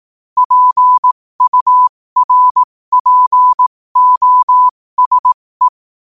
Morse code, heard as a keyed tone, about 18 wpm, spaced as for 15 wpm, 1000 Hz. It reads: PURPOSE